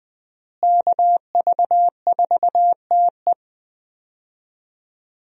Morse code, heard as a keyed tone, about 20 wpm, 700 Hz. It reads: KV4TE
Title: USA Call Signs - 5 Characters - Repeated 2x - 20wpm